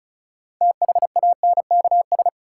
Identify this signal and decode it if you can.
Morse code, heard as a keyed tone, about 36 wpm, 700 Hz. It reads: THANKS
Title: Six Letter Words - 35wpm